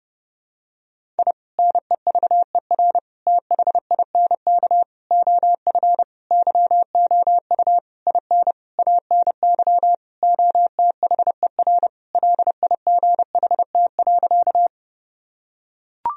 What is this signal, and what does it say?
Morse code, heard as a keyed tone, about 32 wpm, 700 Hz. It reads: I NEVER THINK OF YOU IN ANY OTHER LIGHT.
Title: Sentences from Top 1000 Words - Review Part 1 - 30wpm